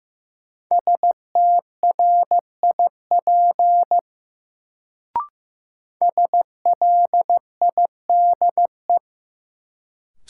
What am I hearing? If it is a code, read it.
Morse code, heard as a keyed tone, about 16 wpm, 700 Hz. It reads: STRIP SLIDE
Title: Sentences from Top 1600 Words - New Word Drill Alt - 15wpm